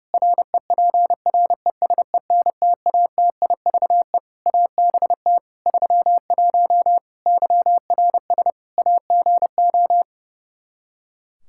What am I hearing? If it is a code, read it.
Morse code, heard as a keyed tone, about 30 wpm, 700 Hz. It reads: REPRESENTATIVE ABT 31 YRS AGO
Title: Practice QSO - Exchange 3 - More Info - 30wpm